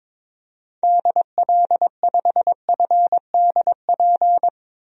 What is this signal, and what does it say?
Morse code, heard as a keyed tone, about 22 wpm, 700 Hz. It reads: DL5FDP
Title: DXCC-100 Call Signs - 6 Characters - 22wpm